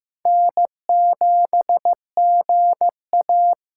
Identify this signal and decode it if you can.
Morse code, heard as a keyed tone, about 15 wpm, 700 Hz. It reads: N7GA
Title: USA Call Signs - 6x Character Spacing - 15wpm